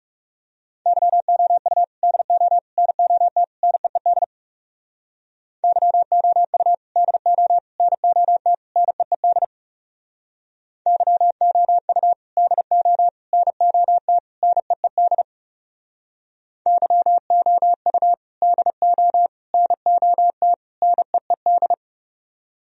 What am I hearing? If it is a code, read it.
Morse code, heard as a keyed tone, about 39 wpm, 700 Hz. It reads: YOU DO NOT NEED YOU DO NOT NEED YOU DO NOT NEED YOU DO NOT NEED